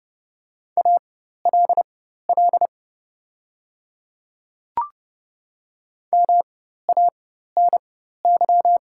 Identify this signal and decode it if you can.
Morse code, heard as a keyed tone, about 30 wpm, spaced as for 14 wpm, 690 Hz. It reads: ALL MANY
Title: 4x Character Spacing - Top 100 Words - 30wpm